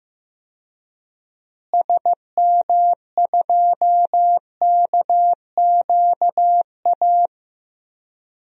Morse code, heard as a keyed tone, about 16 wpm, 700 Hz. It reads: SM2KQA
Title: DXCC-100 Call Signs - 6 Characters - Repeated 3x - 15wpm